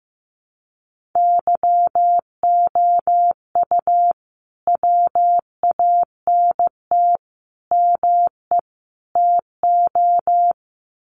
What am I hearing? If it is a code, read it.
Morse code, heard as a keyed tone, about 15 wpm, 700 Hz. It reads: YOU WANT ME TO
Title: Sets of 4 Words 15wpm